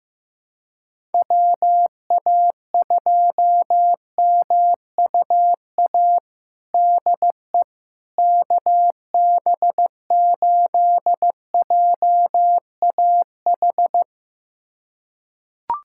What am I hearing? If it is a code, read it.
Morse code, heard as a keyed tone, about 15 wpm, 700 Hz. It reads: WA2MUA DE KB8JAH
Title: USA Callsign DE Callsign 15wpm